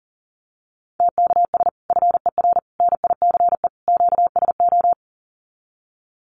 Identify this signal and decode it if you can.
Morse code, heard as a keyed tone, about 40 wpm, 700 Hz. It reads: TKS FER NICE QSO
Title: Common QSO Phrases 40wpm